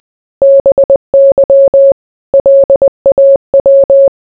Morse code, heard as a keyed tone, about 20 wpm, 550 Hz. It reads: BY LAW